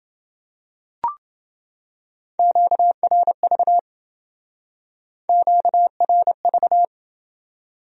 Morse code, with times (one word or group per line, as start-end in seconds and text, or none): 2.4-3.8 QRV
5.3-6.8 QRV